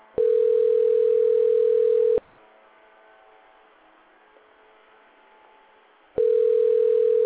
Ringback tone (North America), 440 + 480 Hz.